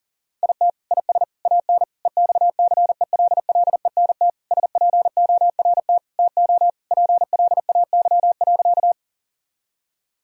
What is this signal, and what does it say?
Morse code, heard as a keyed tone, about 40 wpm, 700 Hz. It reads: IT IS AN EXCELLENT SPORT TO PLAY.
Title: Sentences from Top 1000 Words - Review Part 4 - 40wpm